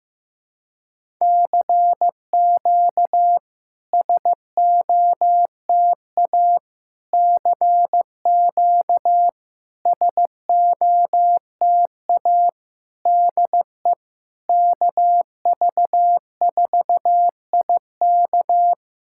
Morse code, heard as a keyed tone, about 15 wpm, 690 Hz. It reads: CQ SOTA CQ SOTA DE KV4IK
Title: SOTA QSO Practice 15wpm